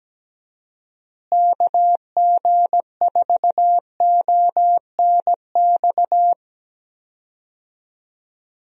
Morse code, heard as a keyed tone, about 17 wpm, 700 Hz. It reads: KG4ONX
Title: USA Call Signs - 6 Characters - Repeated 2x - 17wpm